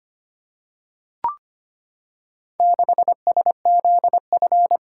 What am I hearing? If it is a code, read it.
Morse code, heard as a keyed tone, about 25 wpm, 700 Hz. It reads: 6SZF
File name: practice / 25wpm / Four Letter-Number Combinations - 25wpm